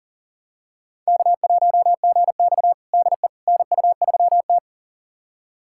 Morse code, heard as a keyed tone, about 40 wpm, 700 Hz. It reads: K1GX DE NU3T
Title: USA Callsign DE Callsign - 4 Characters - 40wpm